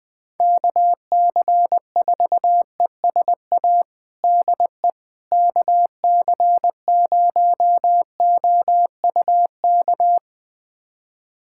Morse code, heard as a keyed tone, about 20 wpm, 700 Hz. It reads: KC4ESA DE KC0OUK
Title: USA Callsign DE Callsign - 6 Characters - 20wpm